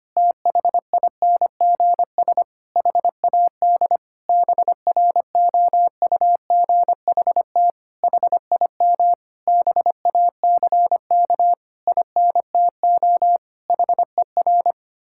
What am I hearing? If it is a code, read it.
Morse code, heard as a keyed tone, about 25 wpm, 700 Hz. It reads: THINGS HAD BROUGHT HIM BACK INTO HER
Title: Sentences from Top 1000 Words - Review Part 6 - 25wpm